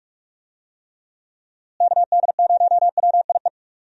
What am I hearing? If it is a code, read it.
Morse code, heard as a keyed tone, about 46 wpm, 700 Hz. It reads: KD0WIE
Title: USA Call Signs - 6 Characters - Repeated 2x - 45wpm